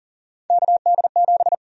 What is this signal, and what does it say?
Morse code, heard as a keyed tone, about 40 wpm, 700 Hz. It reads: KD7
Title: USA Call Sign Prefixes - Repeated 2x - 40wpm